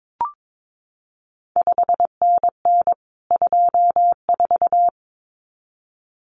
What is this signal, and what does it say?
Morse code, heard as a keyed tone, about 22 wpm, 700 Hz. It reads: 5NN 24
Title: CQ WW DX Contest Exchange - 22wpm